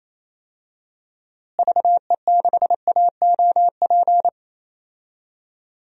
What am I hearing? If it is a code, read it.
Morse code, heard as a keyed tone, about 28 wpm, 700 Hz. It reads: VE6AOP
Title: DXCC-100 Call Signs - 6 Characters - Repeated 2x - 28wpm